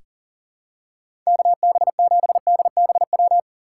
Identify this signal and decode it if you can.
Morse code, heard as a keyed tone, about 40 wpm, 700 Hz. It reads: KB7DBW
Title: USA Call Signs - 6 Characters - 6x Character Spacing - 40wpm